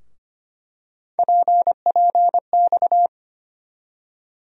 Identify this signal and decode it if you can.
Morse code, heard as a keyed tone, about 25 wpm, 700 Hz. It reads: PPX
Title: USA Call Sign Postfixes - 3 Letters 25wpm